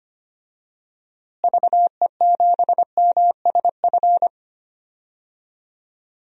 Morse code, heard as a keyed tone, about 25 wpm, 700 Hz. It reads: VE7MSF
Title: DXCC-100 Call Signs - 6 Characters - Repeated 2x - 25wpm